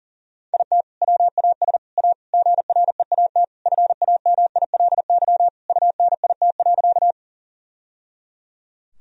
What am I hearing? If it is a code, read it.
Morse code, heard as a keyed tone, about 40 wpm, 700 Hz. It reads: IT WAS A GREAT FAMILY UNIT.
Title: Sentences from Top 1000 Words - Review Part 2 - 40wpm